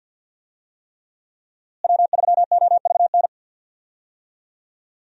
Morse code, heard as a keyed tone, about 53 wpm, 700 Hz. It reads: W3OVN
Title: USA Call Signs - 5 Characters - Repeated 3x - 50wpm